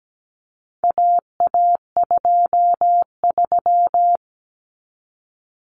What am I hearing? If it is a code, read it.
Morse code, heard as a keyed tone, about 17 wpm, 700 Hz. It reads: AA23
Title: Common Cut Numbers - 4-Digit 17wpm